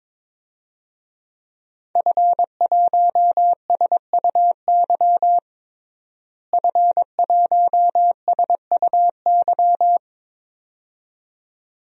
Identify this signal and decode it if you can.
Morse code, heard as a keyed tone, about 22 wpm, 700 Hz. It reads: F1SUY F1SUY
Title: DXCC-100 Call Signs - 5 Characters - Repeated 2x - 22wpm